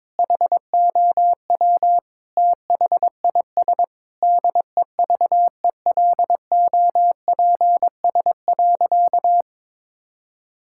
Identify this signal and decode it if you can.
Morse code, heard as a keyed tone, about 22 wpm, 700 Hz. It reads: HOW THIS DEVELOPS.
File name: Sentences with 7-Word Phrases - 22wpm